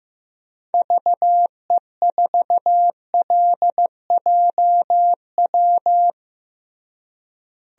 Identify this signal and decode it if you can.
Morse code, heard as a keyed tone, about 15 wpm, 700 Hz. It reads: VE4LJW
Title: DXCC-100 Call Signs - 6 Characters - Repeated 2x - 15wpm